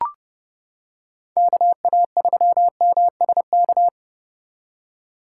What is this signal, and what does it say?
Morse code, heard as a keyed tone, about 30 wpm, 700 Hz. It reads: KA3MSK